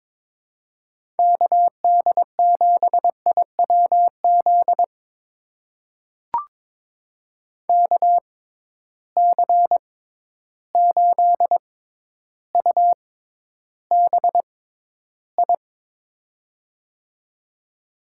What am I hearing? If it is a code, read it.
Morse code, heard as a keyed tone, about 22 wpm, spaced as for 11 wpm, 700 Hz. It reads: KD7IWZ KC8UBI
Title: USA Call Signs - 6 Characters - 6x Character Spacing - 22wpm